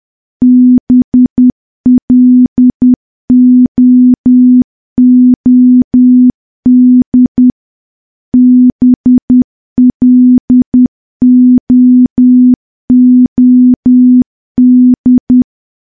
Morse code, heard as a keyed tone, about 10 wpm, 250 Hz. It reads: BLOOD BLOOD
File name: morse / morse-training-10WPM